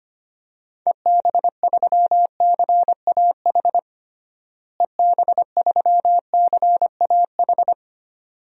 Morse code, heard as a keyed tone, about 25 wpm, 700 Hz. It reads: EB3CAH EB3CAH
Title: DXCC-100 Call Signs - 6 Characters - Repeated 2x - 25wpm